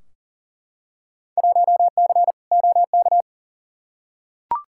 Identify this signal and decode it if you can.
Morse code, heard as a keyed tone, about 40 wpm, 700 Hz. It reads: 1C OK
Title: Field Day Exchanges Encore - 40wpm